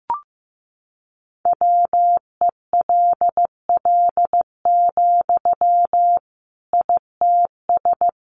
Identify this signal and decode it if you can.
Morse code, heard as a keyed tone, about 15 wpm, 700 Hz. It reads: WELL, ITS
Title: Sentences from Top 600 Words 15wpm